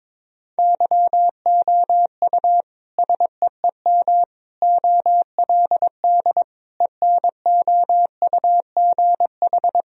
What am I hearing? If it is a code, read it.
Morse code, heard as a keyed tone, about 22 wpm, 700 Hz. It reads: YOU SEEM OLD ENOUGH
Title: Sentences from Top 1000 Words - Review Part 1 - 22wpm